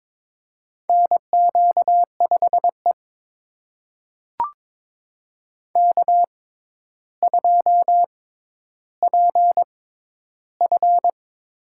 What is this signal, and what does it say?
Morse code, heard as a keyed tone, about 22 wpm, spaced as for 11 wpm, 700 Hz. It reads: NQ5E K2PF
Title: USA Call Signs - 4 Characters - 6x Character Spacing - 22wpm